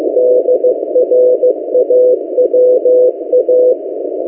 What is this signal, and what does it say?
Morse code, heard as a keyed tone, about 16 wpm, 500 Hz. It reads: DRAWA
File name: M08a- -1355utc-8096khz